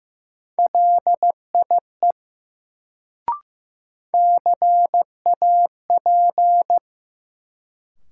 Morse code, heard as a keyed tone, about 15 wpm, 700 Hz. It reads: LIE CAP
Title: Sentences from Top 1600 Words - New Word Drill - 15wpm